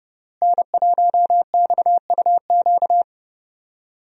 Morse code, heard as a keyed tone, about 28 wpm, 700 Hz. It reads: N1XUQ
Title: USA Call Signs - 5 Characters - Repeated 3x - 30wpm